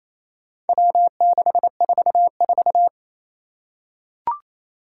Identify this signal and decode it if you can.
Morse code, heard as a keyed tone, about 28 wpm, 700 Hz. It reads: W644
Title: Four Challenging Letter-Number Combinations 28wpm